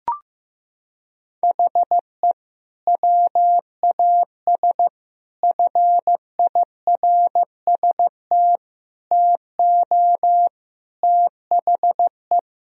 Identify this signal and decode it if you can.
Morse code, heard as a keyed tone, about 15 wpm, 700 Hz. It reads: HE WAS FIRST TO THE
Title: Sentences from Top 1000 Words - Review Part 2 - 15wpm